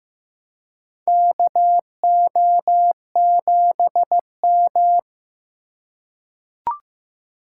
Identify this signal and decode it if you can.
Morse code, heard as a keyed tone, about 15 wpm, 700 Hz. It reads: KO7M
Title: USA Call Signs - 4 Characters - 4x Character Spacing - 15wpm